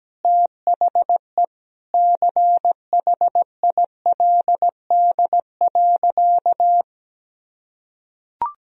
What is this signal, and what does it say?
Morse code, heard as a keyed tone, about 17 wpm, 700 Hz. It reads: THE CHILD.
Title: Sentences from Top 1000 Words - Review Part 1 - 17wpm